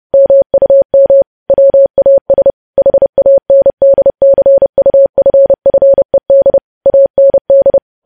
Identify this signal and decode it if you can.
Morse code, heard as a keyed tone, about 30 wpm, 550 Hz. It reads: MUM WAS HANDCUFFED AND
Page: news headlines in morse code at 30 wpm